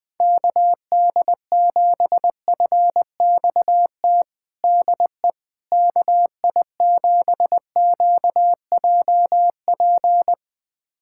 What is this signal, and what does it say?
Morse code, heard as a keyed tone, about 20 wpm, 700 Hz. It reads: KD7FXT DE KI7QJP
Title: USA Callsign DE Callsign - 6 Characters - 20wpm